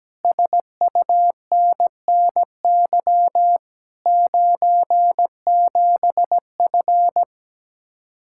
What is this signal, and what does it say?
Morse code, heard as a keyed tone, about 17 wpm, 700 Hz. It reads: SUNNY 97F
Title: Speed-Racing Common QSO Phrases - 17wpm